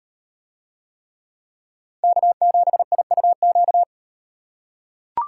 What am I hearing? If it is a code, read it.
Morse code, heard as a keyed tone, about 36 wpm, 700 Hz. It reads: K7IUQ